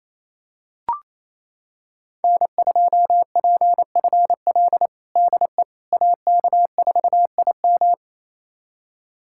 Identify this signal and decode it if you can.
Morse code, heard as a keyed tone, about 28 wpm, 700 Hz. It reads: N2PFL DE AK4IM
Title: USA Callsign DE Callsign - 5 Characters - 28wpm